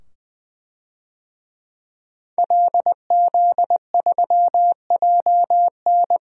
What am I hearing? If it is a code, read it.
Morse code, heard as a keyed tone, about 20 wpm, 700 Hz. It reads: LZ3JN
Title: DXCC-100 Call Signs - 5 Characters - 20wpm